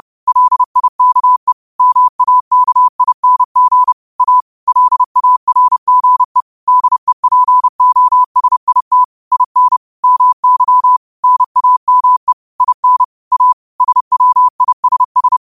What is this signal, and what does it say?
Morse code, heard as a keyed tone, about 30 wpm, 1000 Hz. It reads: LIKE MAKING A LARGE DEPOSIT IN MY NAME IN A SWISS
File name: qotdfiles / 30wpm